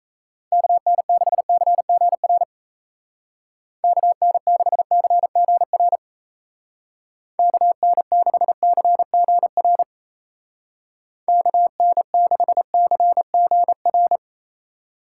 Morse code, heard as a keyed tone, about 36 wpm, 700 Hz. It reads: KN6CGR KN6CGR KN6CGR KN6CGR